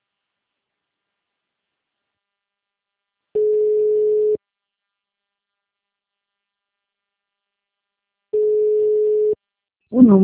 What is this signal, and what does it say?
Ringback tone (Europe, 425 Hz).